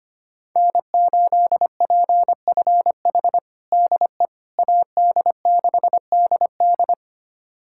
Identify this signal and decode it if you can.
Morse code, heard as a keyed tone, about 25 wpm, 700 Hz. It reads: N8PFH DE AD6DD
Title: USA Callsign DE Callsign - 5 Characters - 25wpm